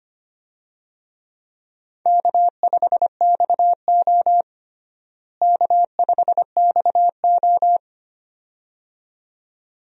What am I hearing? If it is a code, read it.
Morse code, heard as a keyed tone, about 25 wpm, 700 Hz. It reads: K5XO K5XO
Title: USA Call Signs - 4 Characters - Repeated 2x - 25wpm